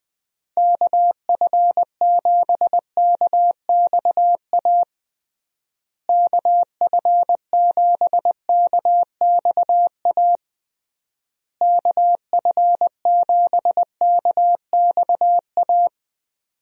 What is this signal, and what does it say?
Morse code, heard as a keyed tone, about 20 wpm, 700 Hz. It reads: KF7KXA KF7KXA KF7KXA
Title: USA Call Signs - 6 Characters - Repeated 3x - 20wpm